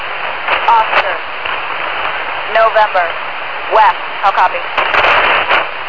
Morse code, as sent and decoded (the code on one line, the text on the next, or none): none